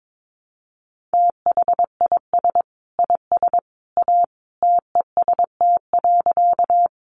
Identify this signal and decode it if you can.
Morse code, heard as a keyed tone, about 22 wpm, 700 Hz. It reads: THIS IS A TEST.
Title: Sentences from Top 500 Words 22wpm